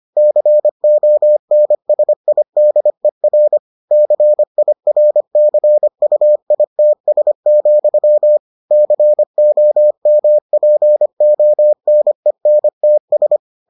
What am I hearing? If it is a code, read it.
Morse code, heard as a keyed tone, about 25 wpm, 600 Hz. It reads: CONSIDER CIRCUITS, COMPONENTS